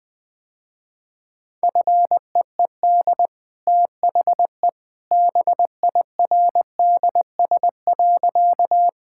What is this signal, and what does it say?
Morse code, heard as a keyed tone, about 20 wpm, 700 Hz. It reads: FEED THE BIRDS.
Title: Easy Sentences with Hints - 20wpm